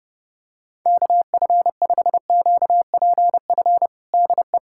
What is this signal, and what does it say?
Morse code, heard as a keyed tone, about 30 wpm, 700 Hz. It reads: KF5QPF DE
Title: USA Callsign DE Callsign 30wpm